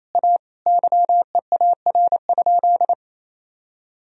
Morse code, heard as a keyed tone, about 28 wpm, 700 Hz. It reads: A YEAR?
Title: Sentences from Top 1000 Words - Review Part 2 - 28wpm